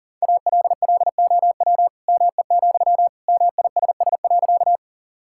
Morse code, heard as a keyed tone, about 40 wpm, 700 Hz. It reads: ALLOW ME, MISS.
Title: Sentences from Top 1000 Words - Review Part 5 - 40wpm